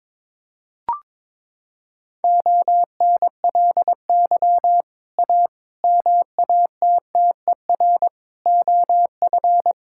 Morse code, heard as a keyed tone, about 22 wpm, 700 Hz. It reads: ONLY A MATTER OF
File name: practice / 22wpm / Sets of 6 Words - 22wpm